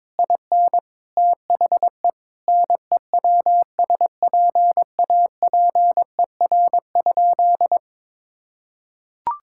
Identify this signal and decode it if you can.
Morse code, heard as a keyed tone, about 22 wpm, 700 Hz. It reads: IN THE NEWSPAPER?
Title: Sentences from Top 1000 Words - Review Part 6 - 22wpm